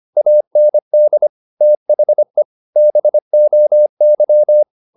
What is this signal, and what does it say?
Morse code, heard as a keyed tone, about 25 wpm, 600 Hz. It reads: AND THE BOY
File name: the old man and the sea part 1 25wpm